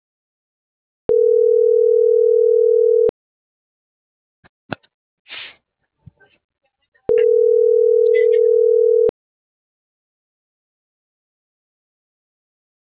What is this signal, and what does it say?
Ringback tone (North America), 440 + 480 Hz.